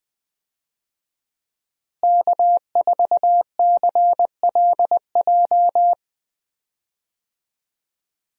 Morse code, heard as a keyed tone, about 20 wpm, 700 Hz. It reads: K4CLJ
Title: USA Call Signs - 5 Characters - Repeated 3x - 20wpm